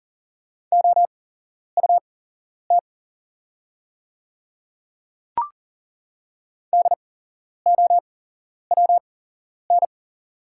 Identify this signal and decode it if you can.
Morse code, heard as a keyed tone, about 40 wpm, spaced as for 11 wpm, 690 Hz. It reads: OUT DOWN
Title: 8x Character Spacing - Top 300 Words - 40wpm